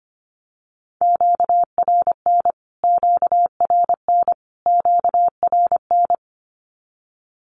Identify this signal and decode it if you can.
Morse code, heard as a keyed tone, about 25 wpm, 700 Hz. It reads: QRN QRN QRN
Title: Q-Codes Repeated 3x 25wpm